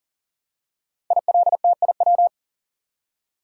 Morse code, heard as a keyed tone, about 38 wpm, 700 Hz. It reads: ILTIW
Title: Code Groups - 5 Letters - 40wpm